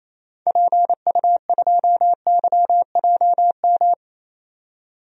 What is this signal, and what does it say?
Morse code, heard as a keyed tone, about 28 wpm, 700 Hz. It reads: PU2YJM
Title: DXCC-100 Call Signs - 6 Characters - 28wpm